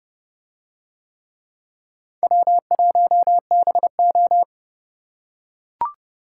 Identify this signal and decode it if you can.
Morse code, heard as a keyed tone, about 29 wpm, 700 Hz. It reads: W1BO